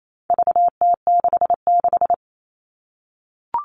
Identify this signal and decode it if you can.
Morse code, heard as a keyed tone, about 28 wpm, 700 Hz. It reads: VT66
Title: All Cut Numbers - 4-Digit 28wpm